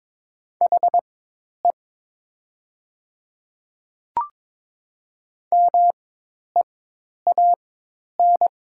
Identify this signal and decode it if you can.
Morse code, heard as a keyed tone, about 22 wpm, spaced as for 10 wpm, 700 Hz. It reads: HE MEAN
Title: 4x Character Spacing - Top 200 Words - 22wpm